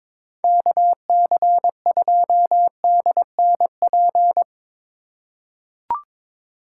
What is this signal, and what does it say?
Morse code, heard as a keyed tone, about 22 wpm, 700 Hz. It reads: KC2DNP
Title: USA Call Signs - 2x Character Spacing - 22wpm